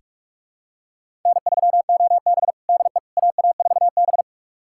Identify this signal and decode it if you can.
Morse code, heard as a keyed tone, about 46 wpm, 700 Hz. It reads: N3OB DE AA4B
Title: USA Callsign DE Callsign - 4 Characters - 45wpm